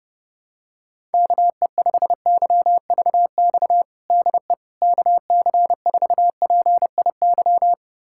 Morse code, heard as a keyed tone, about 30 wpm, 700 Hz. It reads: KE5YVX DE KC4PIY
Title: USA Callsign DE Callsign - 6 Characters - 30wpm